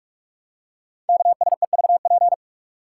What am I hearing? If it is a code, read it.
Morse code, heard as a keyed tone, about 44 wpm, 700 Hz. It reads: KSEVP
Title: Code Groups - 5 Letter:Numbers - 45wpm